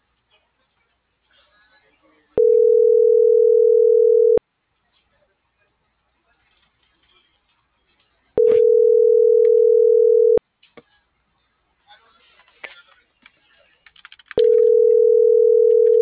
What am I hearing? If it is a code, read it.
Ringback tone (North America), 440 + 480 Hz.